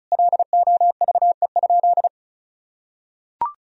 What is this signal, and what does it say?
Morse code, heard as a keyed tone, about 34 wpm, 700 Hz. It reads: LOVE?
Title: Sentences from Top 1000 Words - Review Part 1 - 35wpm